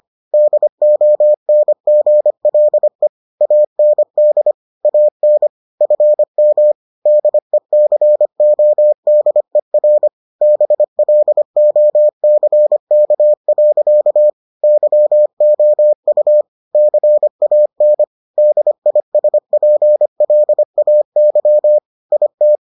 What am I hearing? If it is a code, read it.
Morse code, heard as a keyed tone, about 25 wpm, 600 Hz. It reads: DONGLE AND AN FM DECODER BLOCK. YOU CAN DISPLAY IT